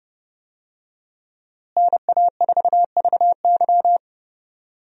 Morse code, heard as a keyed tone, about 32 wpm, 700 Hz. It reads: NA4VY
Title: USA Call Signs - 5 Characters - Repeated 2x - 30wpm